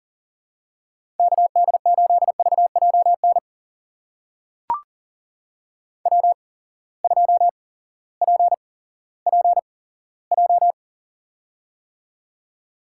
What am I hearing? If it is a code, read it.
Morse code, heard as a keyed tone, about 38 wpm, spaced as for 18 wpm, 700 Hz. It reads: KD8VJN W2PPJ
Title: USA Call Signs - 8x Character Spacing - 40wpm